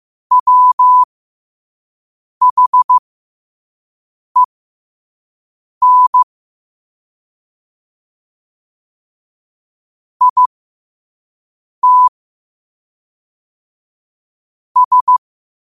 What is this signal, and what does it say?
Morse code, heard as a keyed tone, about 15 wpm, spaced as for 5 wpm, 1000 Hz. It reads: WHEN IT S
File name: qotdfiles / 5wpm